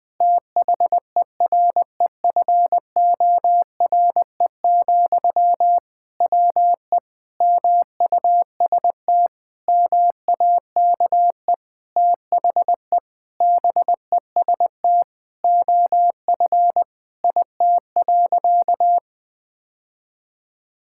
Morse code, heard as a keyed tone, about 20 wpm, 700 Hz. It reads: THEREFORE, WE MUST MAKE THE BEST OF IT.
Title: Sentences from Top 1000 Words - Review Part 5 - 20wpm